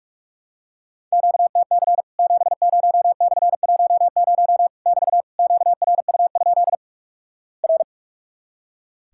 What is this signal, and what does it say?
Morse code, heard as a keyed tone, about 46 wpm, 700 Hz. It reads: QTC 70/10 <BT> QRU? R
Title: Worked All Europe DX Contest - QTC Exchange - 45wpm